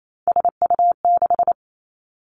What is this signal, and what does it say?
Morse code, heard as a keyed tone, about 28 wpm, 700 Hz. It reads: SU6
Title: Three Challenging Letter-Number Combinations 28wpm